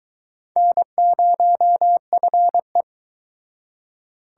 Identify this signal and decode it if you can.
Morse code, heard as a keyed tone, about 22 wpm, 700 Hz. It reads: N0FE